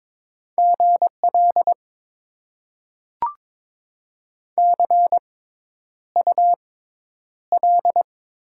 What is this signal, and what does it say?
Morse code, heard as a keyed tone, about 22 wpm, spaced as for 10 wpm, 700 Hz. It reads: GL CUL
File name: Top 90 QSO Elements - 6x Character Spacing - 22wpm